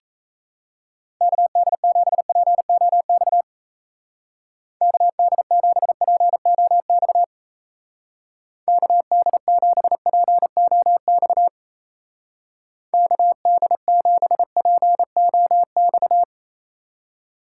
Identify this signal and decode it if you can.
Morse code, heard as a keyed tone, about 34 wpm, 700 Hz. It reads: KD7POX KD7POX KD7POX KD7POX
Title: Speed Racing - USA 2x3 Call Signs 28wpm